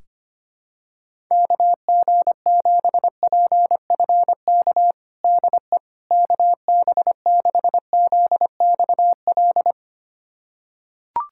Morse code, heard as a keyed tone, about 25 wpm, 700 Hz. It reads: KG7PFK DE KB6ZXL
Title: USA Callsign DE Callsign - 6 Characters - 25wpm